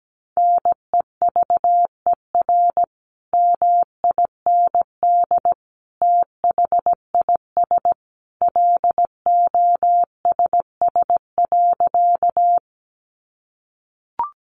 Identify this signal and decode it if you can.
Morse code, heard as a keyed tone, about 17 wpm, 700 Hz. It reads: NEVER MIND THIS LOSS.
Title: Sentences from Top 1400 Words 17wpm